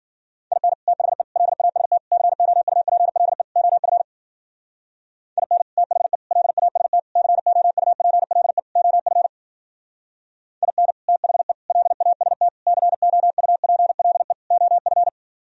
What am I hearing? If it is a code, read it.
Morse code, heard as a keyed tone, about 50 wpm, 700 Hz. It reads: IN THE LAST COUPLE OF IN THE LAST COUPLE OF IN THE LAST COUPLE OF